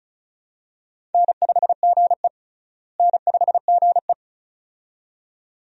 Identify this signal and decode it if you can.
Morse code, heard as a keyed tone, about 34 wpm, 700 Hz. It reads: N5GE N5GE
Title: USA Call Signs - 4 Characters - Repeated 2x - 35wpm